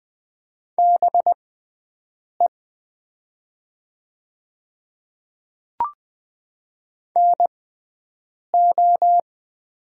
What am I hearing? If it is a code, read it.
Morse code, heard as a keyed tone, about 20 wpm, spaced as for 7 wpm, 700 Hz. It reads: BE NO
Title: 6x Character Spacing - Top 100 Words - 20wpm